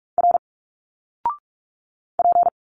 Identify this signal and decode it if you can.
Morse code, heard as a keyed tone, about 46 wpm, 700 Hz. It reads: R P
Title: Single Letters 45wpm